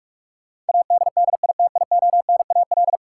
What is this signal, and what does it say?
Morse code, heard as a keyed tone, about 46 wpm, 700 Hz. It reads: ADDITIONAL
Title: Sentences from Top 1400 Words - New Word Drill Alt - 45wpm